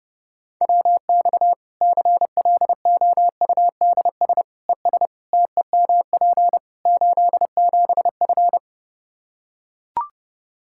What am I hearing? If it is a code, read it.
Morse code, heard as a keyed tone, about 30 wpm, 700 Hz. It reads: WX CLOUDS ES TEMP 87F
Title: Practice QSO - Exchange 2 - Info - 30wpm